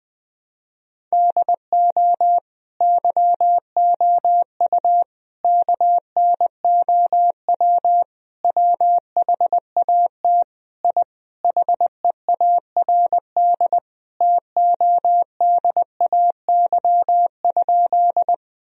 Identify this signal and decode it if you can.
Morse code, heard as a keyed tone, about 20 wpm, 700 Hz. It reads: DO YOU KNOW WHAT I HEARD TODAY?
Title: Sentences from Top 1000 Words - Review Part 4 - 20wpm